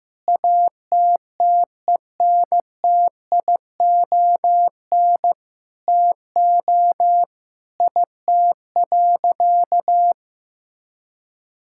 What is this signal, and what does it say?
Morse code, heard as a keyed tone, about 15 wpm, 700 Hz. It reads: ATTENTION TO IT.
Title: Sentences from Top 1000 Words - Review Part 6 - 15wpm